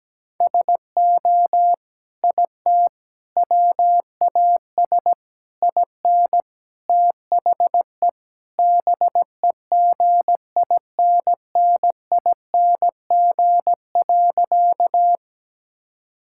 Morse code, heard as a keyed tone, about 17 wpm, 700 Hz. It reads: SO IT WAS IN THE BEGINNING.
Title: Sentences from Top 1000 Words - Review Part 5 - 17wpm